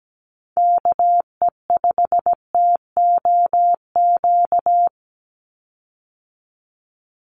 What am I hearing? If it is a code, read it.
Morse code, heard as a keyed tone, about 17 wpm, 700 Hz. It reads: KE5TOQ
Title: USA Call Signs 17wpm